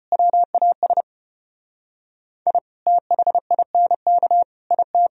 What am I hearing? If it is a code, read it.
Morse code, heard as a keyed tone, about 33 wpm, 700 Hz. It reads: WAS I THINK IT